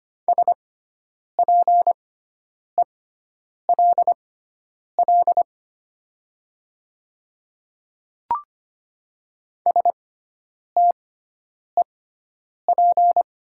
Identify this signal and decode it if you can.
Morse code, heard as a keyed tone, about 25 wpm, spaced as for 9 wpm, 700 Hz. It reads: SPELL STEP
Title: 6x Character Spacing - Top 500 Words - 25wpm